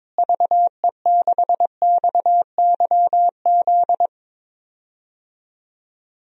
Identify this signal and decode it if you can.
Morse code, heard as a keyed tone, about 21 wpm, 700 Hz. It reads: VE6XYZ
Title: DXCC-100 Call Signs - 6 Characters - Repeated 3x - 22wpm